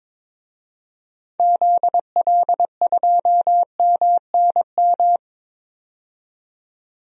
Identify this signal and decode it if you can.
Morse code, heard as a keyed tone, about 22 wpm, 700 Hz. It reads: ZL2MNM
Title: DXCC-100 Call Signs - 6 Characters - 22wpm